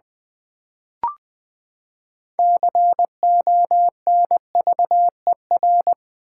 Morse code, heard as a keyed tone, about 19 wpm, 700 Hz. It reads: CONVER
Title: Build A Word - 6 Letters At A Time - 12-Letter Words - 20wpm